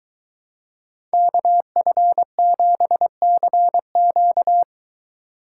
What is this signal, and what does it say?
Morse code, heard as a keyed tone, about 23 wpm, 700 Hz. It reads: KF7CQ